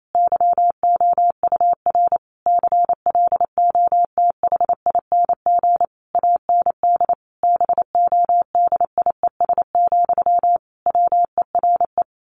Morse code, heard as a keyed tone, about 28 wpm, 700 Hz. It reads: YOUR CLOTHING AND BODIES, WERE
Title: A Sound Of Thunder 28wpm